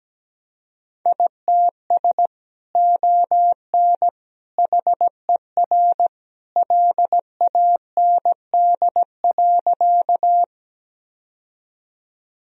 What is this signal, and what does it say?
Morse code, heard as a keyed tone, about 17 wpm, 700 Hz. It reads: ITS ON HER LAND.
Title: Sentences from Top 1000 Words - Review Part 1 - 17wpm